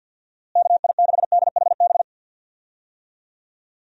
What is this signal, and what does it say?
Morse code, heard as a keyed tone, about 50 wpm, 700 Hz. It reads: KI6DHB
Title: USA Call Signs - 6 Characters - Repeated 3x - 50wpm